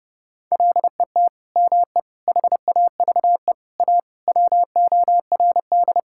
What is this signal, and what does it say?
Morse code, heard as a keyed tone, about 30 wpm, 700 Hz. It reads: LET ME HAVE A WORD
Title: Sentences from Top 1000 Words - Review Part 6 - 30wpm